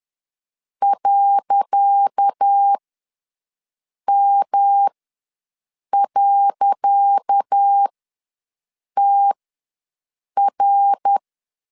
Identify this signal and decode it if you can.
Morse code, heard as a keyed tone, about 10 wpm, spaced as for 6 wpm, 800 Hz. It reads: .M.TR
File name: LEZIONI / 10 SIGNS / 10C DOT ws 10wpm 4213